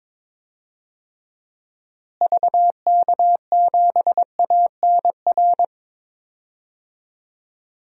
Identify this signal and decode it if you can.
Morse code, heard as a keyed tone, about 22 wpm, 700 Hz. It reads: VK7ANR
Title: DXCC-100 Call Signs - 6 Characters - Repeated 2x - 22wpm